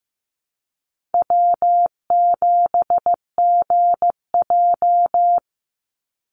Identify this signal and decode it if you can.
Morse code, heard as a keyed tone, about 15 wpm, 700 Hz. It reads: W7GJ
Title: NA SOTA Chaser Callsigns 15wpm